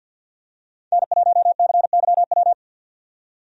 Morse code, heard as a keyed tone, about 45 wpm, 700 Hz. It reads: N1CYW
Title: USA Call Signs - 5 Characters - 50wpm